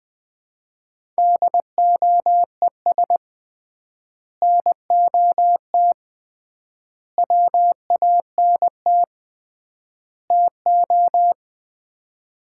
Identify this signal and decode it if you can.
Morse code, heard as a keyed tone, about 20 wpm, 700 Hz. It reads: DOES NOT WANT TO
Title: 3x Word-Spacing - Sets of 4 Words - 20wpm